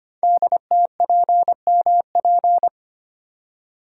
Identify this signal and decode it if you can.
Morse code, heard as a keyed tone, about 25 wpm, 700 Hz. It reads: DTPMP
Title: Code Groups - 5 Letters - 25wpm